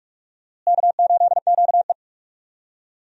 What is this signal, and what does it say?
Morse code, heard as a keyed tone, about 42 wpm, 700 Hz. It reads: K8QE
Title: USA Call Signs - 4 Characters - Repeated 2x - 45wpm